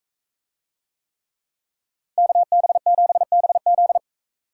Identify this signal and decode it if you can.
Morse code, heard as a keyed tone, about 44 wpm, 700 Hz. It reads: KB7BZ